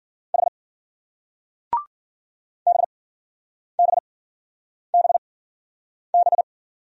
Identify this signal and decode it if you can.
Morse code, heard as a keyed tone, about 56 wpm, spaced as for 9 wpm, 700 Hz. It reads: H BBBB